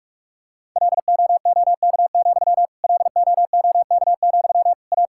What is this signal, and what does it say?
Morse code, heard as a keyed tone, about 43 wpm, 700 Hz. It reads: LOOK, LOOK, A
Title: Sentences from Top 1000 Words - Review Part 5 - 45wpm